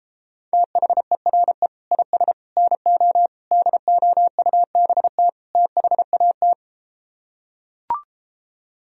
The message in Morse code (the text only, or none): THERE IS NO DOUBT THAT